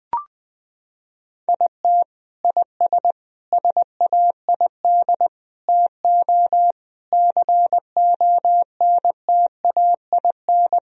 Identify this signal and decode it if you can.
Morse code, heard as a keyed tone, about 20 wpm, 700 Hz. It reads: IT IS SAID TO CONTAIN
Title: Sentences from Top 1000 Words - Review Part 2 - 20wpm